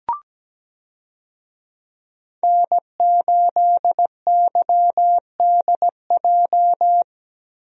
Morse code, heard as a keyed tone, about 17 wpm, 700 Hz. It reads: N8YDJ
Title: USA Call Signs - 5 Characters - Repeated 3x - 17wpm